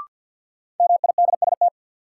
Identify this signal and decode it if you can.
Morse code, heard as a keyed tone, about 52 wpm, 700 Hz. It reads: MIDST
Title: Sets of 4 Words - New Word Drill - 50wpm